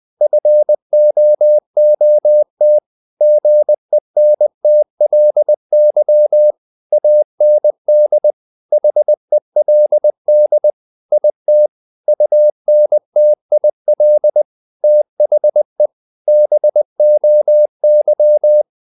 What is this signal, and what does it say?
Morse code, heard as a keyed tone, about 20 wpm, 600 Hz. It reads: FOOT GENTLY AND HELD IT UNTIL THE BOY
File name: the old man and the sea / the old man and the sea part 1 20wpm